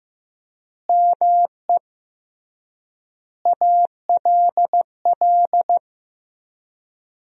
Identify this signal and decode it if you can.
Morse code, heard as a keyed tone, about 15 wpm, 700 Hz. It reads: ME ALL
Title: 3x Word-Spacing - Sentences from Top 200 Words - 15wpm